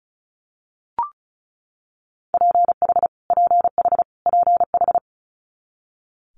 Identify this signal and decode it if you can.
Morse code, heard as a keyed tone, about 36 wpm, 700 Hz. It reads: PH PH PH
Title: Multi-letter Phonemes Repeated 3x 35wpm